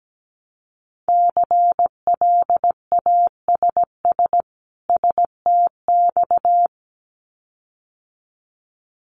Morse code, heard as a keyed tone, about 17 wpm, 700 Hz. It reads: CLASS STX
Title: Winter Field Day Exchanges 17wpm